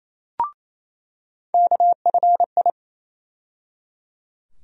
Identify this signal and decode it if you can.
Morse code, heard as a keyed tone, about 28 wpm, 700 Hz. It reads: KFI
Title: USA Call Sign Postfixes - 3 Letters 28wpm